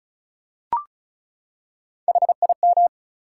Morse code, heard as a keyed tone, about 34 wpm, 700 Hz. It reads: HIM
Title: Three Letter Words - 35wpm